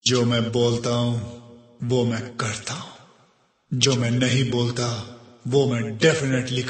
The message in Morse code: none